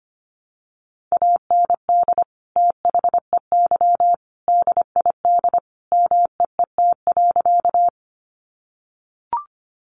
Morse code, heard as a keyed tone, about 25 wpm, 690 Hz. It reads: AND THEY DID MEET.
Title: Sentences from Top 1100 Words 25wpm